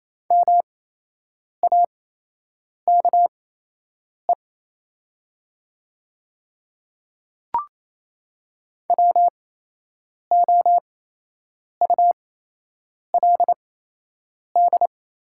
Morse code, heard as a keyed tone, about 28 wpm, spaced as for 8 wpm, 700 Hz. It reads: MAKE WOULD
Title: 8x Character Spacing - Top 100 Words - 28wpm